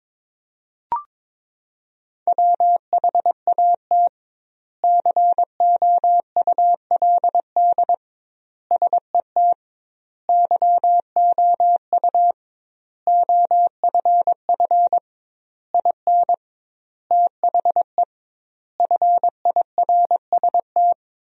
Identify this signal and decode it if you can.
Morse code, heard as a keyed tone, about 22 wpm, 700 Hz. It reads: WHAT COULD SET YOU OFF IN THE FIRST
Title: 2x Word-Spacing - Sentences from Top 200 Words - 22wpm